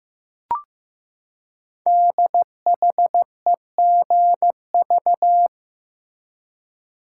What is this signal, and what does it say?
Morse code, heard as a keyed tone, about 15 wpm, 700 Hz. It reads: DHEGV
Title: Code Groups - 5 Letters - 15wpm